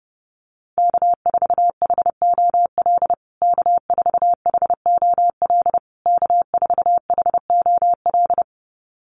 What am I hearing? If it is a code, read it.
Morse code, heard as a keyed tone, about 28 wpm, 700 Hz. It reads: K4HOL K4HOL K4HOL
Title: USA Call Signs Repeated 3x 30wpm